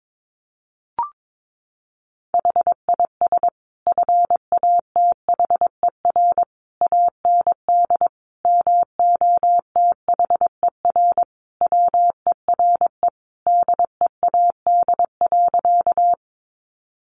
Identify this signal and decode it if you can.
Morse code, heard as a keyed tone, about 22 wpm, 700 Hz. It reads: HIS FATHER AND MOTHER WERE DEAD.
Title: Sentences from Top 1500 Words 22wpm